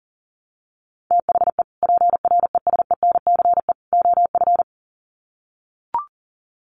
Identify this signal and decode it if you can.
Morse code, heard as a keyed tone, about 40 wpm, 700 Hz. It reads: THE PRESENCE OF
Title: Sets of 3 Words 40wpm